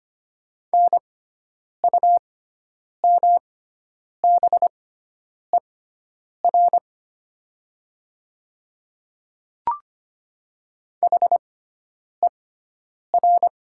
Morse code, heard as a keyed tone, about 25 wpm, spaced as for 9 wpm, 690 Hz. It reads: NUMBER HER
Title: 6x Character Spacing - Top 100 Words - 25wpm